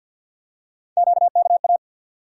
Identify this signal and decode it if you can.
Morse code, heard as a keyed tone, about 51 wpm, 700 Hz. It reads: QKA
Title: USA Call Sign Postfixes - 3 Letters 50wpm